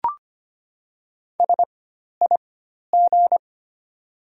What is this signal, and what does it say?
Morse code, heard as a keyed tone, about 25 wpm, spaced as for 12 wpm, 700 Hz. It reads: SIG
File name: Top 90 QSO Elements - 4x Character Spacing - 25wpm